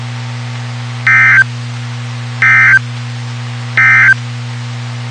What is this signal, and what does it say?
Emergency Alert System (SAME) tones, heard as harsh buzzing bursts. It is heard from 1.0 to 4.2 s.
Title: WXK91 NOAA Weather Radio